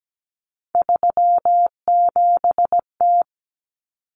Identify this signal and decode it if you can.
Morse code, heard as a keyed tone, about 17 wpm, 700 Hz. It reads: 37T
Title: Common Cut Numbers - 3-Digit 17wpm